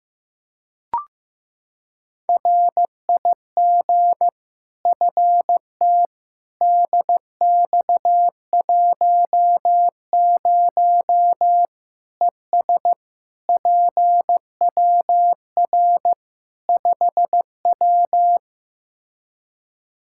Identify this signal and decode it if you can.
Morse code, heard as a keyed tone, about 15 wpm, 700 Hz. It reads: RIG FT DX10 ES PWR 5W
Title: Practice QSO - Exchange 2 - Info - 15wpm